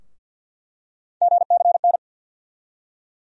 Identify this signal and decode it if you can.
Morse code, heard as a keyed tone, about 46 wpm, 700 Hz. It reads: GCN
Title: USA Call Sign Postfixes - 3 Letters 50wpm